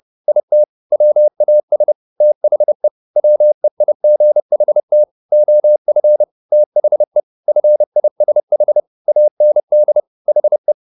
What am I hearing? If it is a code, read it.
Morse code, heard as a keyed tone, about 30 wpm, 600 Hz. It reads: IT WAS THE WEIGHT OF THE FISH AND HE
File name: the old man and the sea / the old man and the sea 1 30wpm